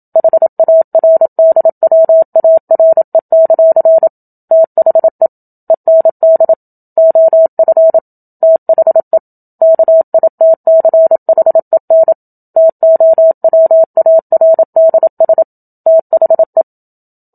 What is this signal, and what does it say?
Morse code, heard as a keyed tone, about 27 wpm, 650 Hz. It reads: HARDWARE; THE END OF THE KITCHEN TOWARDS THE
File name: Book Two / 01-Chapter 1 - 27 WPM